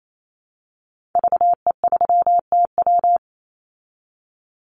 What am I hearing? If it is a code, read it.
Morse code, heard as a keyed tone, about 28 wpm, 700 Hz. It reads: VE3TW
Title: Call Signs Repeated 3x - 28wpm